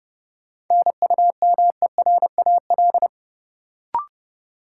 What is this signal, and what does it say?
Morse code, heard as a keyed tone, about 31 wpm, 700 Hz. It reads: NUMERAL
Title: Sentences from Top 1000 Words - Review Part 2 - New Word Drill - 30wpm